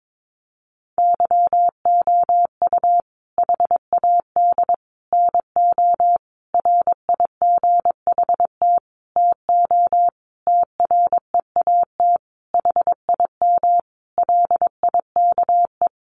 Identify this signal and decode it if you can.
Morse code, heard as a keyed tone, about 22 wpm, 700 Hz. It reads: YOU HAD NO RIGHT TO TREAT HIM LIKE
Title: Sentences from Top 1300 Words 22wpm